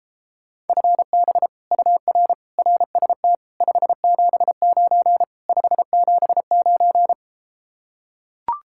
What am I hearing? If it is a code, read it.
Morse code, heard as a keyed tone, about 33 wpm, 700 Hz. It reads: FB UR RST 579 579